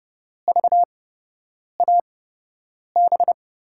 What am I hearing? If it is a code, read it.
Morse code, heard as a keyed tone, about 31 wpm, spaced as for 8 wpm, 700 Hz. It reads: VAB
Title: USA Call Signs - 5 Characters - 8x Character Spacing - 30wpm